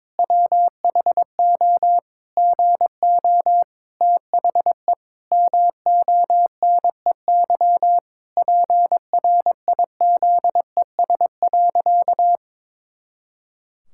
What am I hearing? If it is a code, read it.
Morse code, heard as a keyed tone, about 22 wpm, 700 Hz. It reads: WHO GO THE MONEY PRIZES.
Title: Pangram Challenge v2 - 22wpm